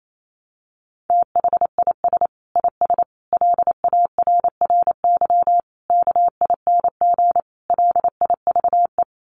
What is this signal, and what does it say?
Morse code, heard as a keyed tone, about 28 wpm, 700 Hz. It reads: THIS IS LARRY KING LIVE
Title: Sets of 5 Words Encore2 28wpm